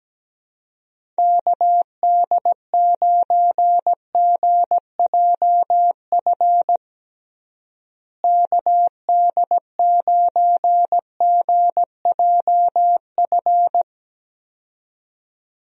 Morse code, heard as a keyed tone, about 17 wpm, 700 Hz. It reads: KD9GJF KD9GJF
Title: USA Call Signs - 6 Characters - Repeated 2x - 17wpm